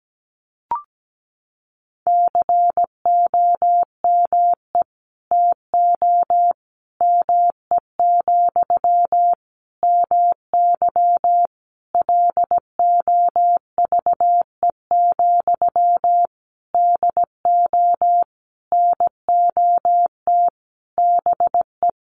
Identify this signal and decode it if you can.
Morse code, heard as a keyed tone, about 17 wpm, 700 Hz. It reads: COME TO ME, MY LOVE, DO NOT BE
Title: Sentences from Top 1500 Words 17wpm